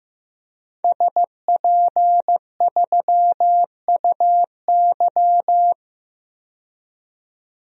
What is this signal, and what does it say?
Morse code, heard as a keyed tone, about 15 wpm, 700 Hz. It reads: SP3UY
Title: DXCC-100 Call Signs - 5 Characters - 15wpm